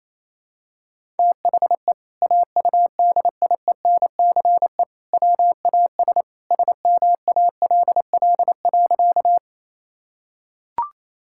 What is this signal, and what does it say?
Morse code, heard as a keyed tone, about 28 wpm, 700 Hz. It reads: THE AUDIENCE WAS SMALL.
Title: Sentences from Top 1000 Words - Review Part 5 - 28wpm